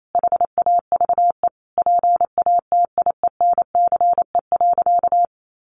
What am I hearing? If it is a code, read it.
Morse code, heard as a keyed tone, about 28 wpm, 700 Hz. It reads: HAVE PATIENCE.
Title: Sentences from Top 1300 Words 28wpm